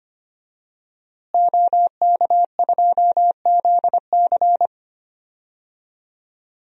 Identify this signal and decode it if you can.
Morse code, heard as a keyed tone, about 25 wpm, 700 Hz. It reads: OK2ZC
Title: DXCC-100 Call Signs - 5 Characters - Repeated 2x - 25wpm